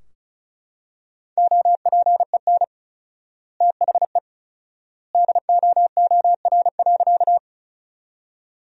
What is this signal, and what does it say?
Morse code, heard as a keyed tone, about 36 wpm, 700 Hz. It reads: OPEN THE DOOR.
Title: 4x Word-Spacing - Sentences from Top 300 Words - 35wpm